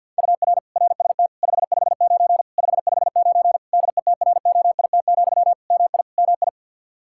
Morse code, heard as a keyed tone, about 52 wpm, 700 Hz. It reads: UR RST 559 559 DETROIT, MI MI
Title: Practice QSO - Exchange 1 - Beginning - 50wpm